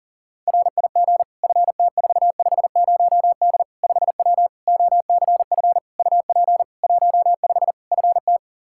Morse code, heard as a keyed tone, about 40 wpm, 700 Hz. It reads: RIG FT450D 5W OCF UP 15 FT
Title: K1USN SST QSO Practice 40wpm